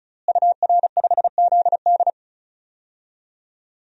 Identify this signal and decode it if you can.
Morse code, heard as a keyed tone, about 36 wpm, 700 Hz. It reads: UR5ZD
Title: DXCC-100 Call Signs - 5 Characters - Repeated 2x - 35wpm